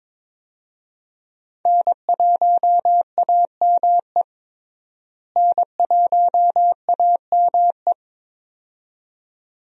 Morse code, heard as a keyed tone, about 22 wpm, 700 Hz. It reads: N1AME N1AME
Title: USA Call Signs - 5 Characters - Repeated 2x - 22wpm